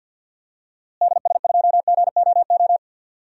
Morse code, heard as a keyed tone, about 52 wpm, 700 Hz. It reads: DS2GOO
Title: DXCC-100 Call Signs - 6 Characters - Repeated 3x - 50wpm